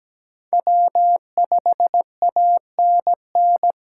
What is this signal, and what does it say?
Morse code, heard as a keyed tone, about 20 wpm, 700 Hz. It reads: W5ANN
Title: Speed Racing - USA 1x3 Call Signs 17wpm